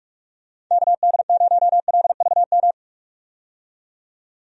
Morse code, heard as a keyed tone, about 44 wpm, 700 Hz. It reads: KD0LVM
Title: USA Call Signs - 6 Characters - 45wpm